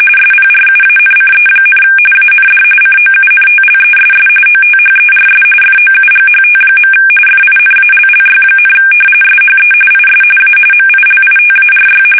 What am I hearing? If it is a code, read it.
RTTY, about 45.45 baud. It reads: SPRUANCE, AND COUNTLESS OTHERS FOUGHT. FINALLY, SHE IS A MILITARY
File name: fsk850 / battleshipnj